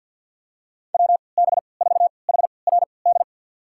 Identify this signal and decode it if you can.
Morse code, heard as a keyed tone, about 50 wpm, spaced as for 28 wpm, 700 Hz. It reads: WB4HRD
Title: USA Call Signs - 6 Characters - 3x Character Spacing - 50wpm